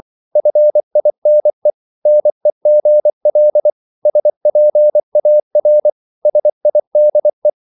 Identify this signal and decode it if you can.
Morse code, heard as a keyed tone, about 24 wpm, 600 Hz. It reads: FINE NEGL SPAR SIDE
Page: Korte danske ord 24wpm